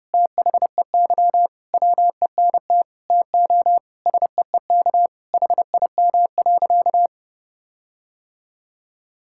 Morse code, heard as a keyed tone, about 32 wpm, 700 Hz. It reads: THEY WENT TO SEEK HIM.
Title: Sentences from Top 1600 Words 30wpm